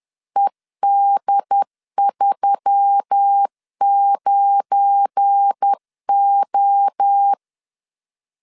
Morse code, heard as a keyed tone, about 10 wpm, 800 Hz. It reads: ED39O
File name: CW / DET / 1 9 1500C S A 10wpm 4511